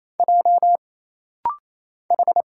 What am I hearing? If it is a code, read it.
Morse code, heard as a keyed tone, about 27 wpm, 700 Hz. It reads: J H
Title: Single Letters - Rapid-Fire 28wpm